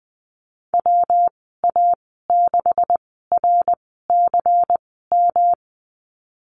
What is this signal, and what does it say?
Morse code, heard as a keyed tone, about 20 wpm, spaced as for 15 wpm, 700 Hz. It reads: WA6RCM
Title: 154 Lesson 25g - Letter p - Call Signs 20wpm